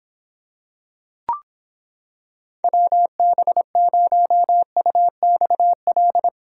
Morse code, heard as a keyed tone, about 26 wpm, 700 Hz. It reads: WB0UXL